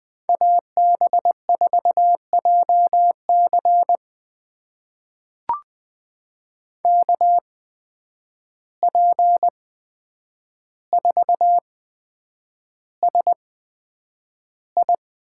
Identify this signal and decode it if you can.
Morse code, heard as a keyed tone, about 20 wpm, spaced as for 9 wpm, 700 Hz. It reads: AB4JC KP4SI
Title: USA Call Signs - 5 Characters - 8x Character Spacing - 20wpm